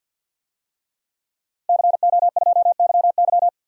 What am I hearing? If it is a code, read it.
Morse code, heard as a keyed tone, about 52 wpm, 700 Hz. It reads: CO2YY
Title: DXCC-100 Call Signs - 5 Characters - Repeated 3x - 50wpm